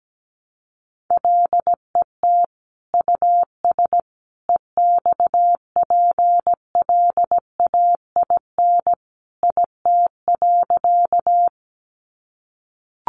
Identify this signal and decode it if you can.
Morse code, heard as a keyed tone, about 17 wpm, 700 Hz. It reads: LET US EXPLAIN IT.
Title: Sentences from Top 1100 Words 17wpm